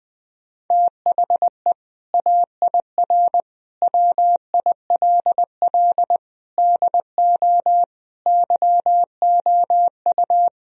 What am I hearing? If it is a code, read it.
Morse code, heard as a keyed tone, about 20 wpm, 700 Hz. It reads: THE AIR WILL DO YOU
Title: Sentences from Top 1000 Words - Review Part 1 - 20wpm